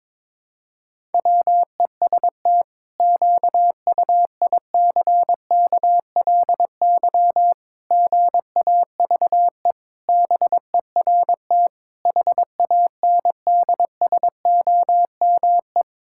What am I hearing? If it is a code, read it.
Morse code, heard as a keyed tone, about 22 wpm, 700 Hz. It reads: WEST QUICKLY GAVE BERT HANDSOME
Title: Pangram Challenge v2 - 22wpm